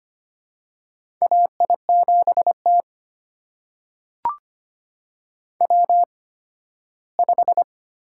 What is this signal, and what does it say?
Morse code, heard as a keyed tone, about 25 wpm, spaced as for 15 wpm, 700 Hz. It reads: AI7T W5
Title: USA Call Signs - 4 Characters - 8x Character Spacing - 25wpm